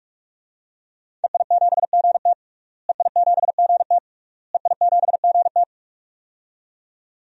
Morse code, heard as a keyed tone, about 46 wpm, 700 Hz. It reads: EI7GT EI7GT EI7GT
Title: DXCC-100 Call Signs - 5 Characters - Repeated 3x - 45wpm